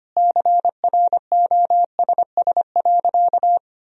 Morse code, heard as a keyed tone, about 25 wpm, 700 Hz. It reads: CROSS.
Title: Sentences from Top 1000 Words - Review Part 1 - 25wpm